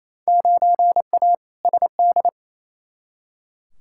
Morse code, heard as a keyed tone, about 28 wpm, 700 Hz. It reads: 9A SD
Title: Field Day Exchanges Encore - 28wpm